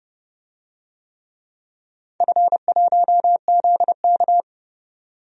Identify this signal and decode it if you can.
Morse code, heard as a keyed tone, about 32 wpm, 700 Hz. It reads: F1ZK
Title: DXCC-100 Call Signs - 4 Characters - 30wpm